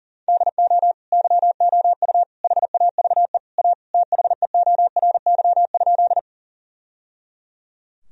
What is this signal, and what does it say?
Morse code, heard as a keyed tone, about 40 wpm, 700 Hz. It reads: DO YOU HAVE A THEORY?
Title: Sentences from Top 1000 Words - Review Part 3 - 40wpm